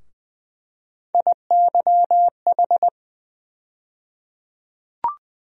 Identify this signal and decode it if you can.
Morse code, heard as a keyed tone, about 20 wpm, 700 Hz. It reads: IYH
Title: USA Call Sign Postfixes - 3 Letters 20wpm